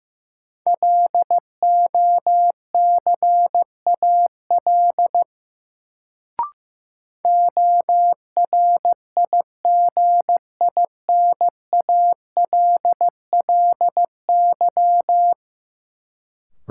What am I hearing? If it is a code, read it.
Morse code, heard as a keyed tone, about 15 wpm, 700 Hz. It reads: LOCAL ORIGINALLY
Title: Sentences from Top 1500 Words - New Word Drill Alt - 15wpm